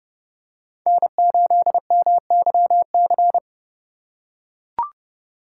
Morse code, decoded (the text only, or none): N8MYC